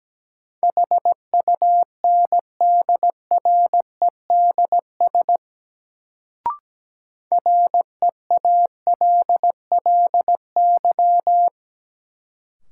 Morse code, heard as a keyed tone, about 17 wpm, 700 Hz. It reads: HUNDREDS REALLY
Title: Sets of 4 Words - New Word Drill - 17wpm